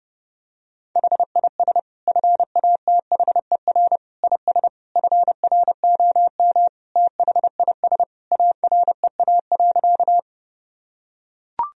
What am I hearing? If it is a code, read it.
Morse code, heard as a keyed tone, about 30 wpm, 700 Hz. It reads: HIS FATHER IS FROM THIS AREA.
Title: Sentences from Top 1000 Words - Review Part 2 - 30wpm